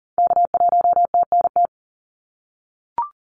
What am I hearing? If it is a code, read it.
Morse code, heard as a keyed tone, about 40 wpm, 700 Hz. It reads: K1TNT
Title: NA SOTA Activator Callsigns 40wpm